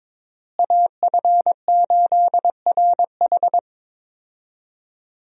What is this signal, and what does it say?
Morse code, heard as a keyed tone, about 22 wpm, 700 Hz. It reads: AF8RH
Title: USA Call Signs - 5 Characters - Repeated 2x - 22wpm